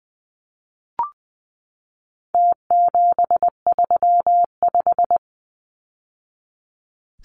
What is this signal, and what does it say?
Morse code, heard as a keyed tone, about 20 wpm, 700 Hz. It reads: T735
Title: Numbers - Contest Style - 4-Digit 20wpm